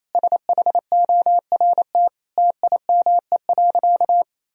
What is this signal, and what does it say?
Morse code, heard as a keyed tone, about 28 wpm, 700 Hz. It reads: SHORT TIME.
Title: Sentences from Top 1000 Words - Review Part 3 - 28wpm